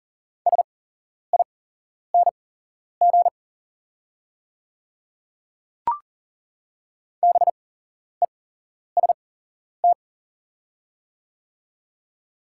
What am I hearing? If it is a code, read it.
Morse code, heard as a keyed tone, about 40 wpm, spaced as for 11 wpm, 690 Hz. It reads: SING BEST
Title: 8x Character Spacing - Top 500 Words - 40wpm